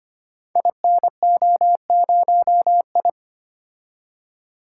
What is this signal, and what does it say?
Morse code, heard as a keyed tone, about 25 wpm, 700 Hz. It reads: INO0I
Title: Code Groups - 5 Letter:Numbers - 25wpm